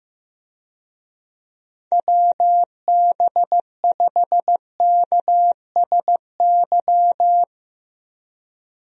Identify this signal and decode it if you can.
Morse code, heard as a keyed tone, about 15 wpm, 700 Hz. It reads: WB5KSY
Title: USA Call Signs - 6 Characters - 15wpm